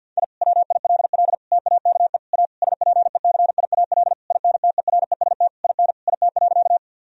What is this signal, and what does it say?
Morse code, heard as a keyed tone, about 50 wpm, 700 Hz. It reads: I WILL TAKE A SPECIAL INTEREST IN IT.
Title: Sentences from Top 1000 Words - Review Part 3 - 50wpm